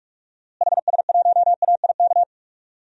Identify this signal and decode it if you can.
Morse code, heard as a keyed tone, about 46 wpm, 700 Hz. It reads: HS1AIK
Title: DXCC-100 Call Signs - 6 Characters - Repeated 2x - 45wpm